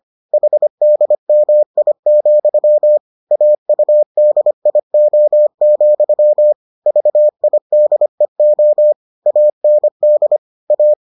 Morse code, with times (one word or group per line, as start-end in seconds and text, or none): 0.3-3.0 HDMI,
3.3-6.5 AUDIO,
6.9-8.9 VIDEO
9.3-10.4 AND
10.7-10.9 A